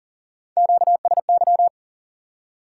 Morse code, heard as a keyed tone, about 40 wpm, 700 Hz. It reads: QSY
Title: Top 180 QSO Elements 40wpm